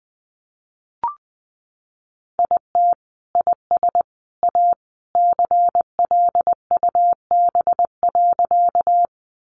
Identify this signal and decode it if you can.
Morse code, heard as a keyed tone, about 20 wpm, 700 Hz. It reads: IT IS A CLUB.
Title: Sentences from Top 1200 Words 20wpm